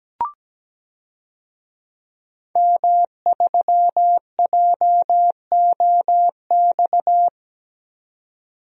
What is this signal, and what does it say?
Morse code, heard as a keyed tone, about 17 wpm, 700 Hz. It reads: M3JOX
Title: DXCC-100 Call Signs - 5 Characters - 17wpm